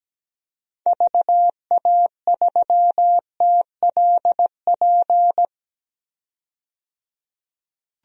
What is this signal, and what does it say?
Morse code, heard as a keyed tone, about 17 wpm, 700 Hz. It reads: VA3TLP
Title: DXCC-100 Call Signs - 6 Characters - 17wpm